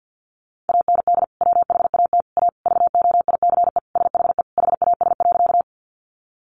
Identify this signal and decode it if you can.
Morse code, heard as a keyed tone, about 50 wpm, 700 Hz. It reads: AND WHAT A VOICE SHE HAS.
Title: Sentences from Top 500 Words 50wpm